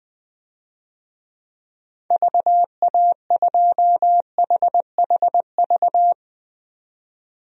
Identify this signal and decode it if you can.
Morse code, heard as a keyed tone, about 20 wpm, 700 Hz. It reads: VA2HHV
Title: DXCC-100 Call Signs - 6 Characters - 20wpm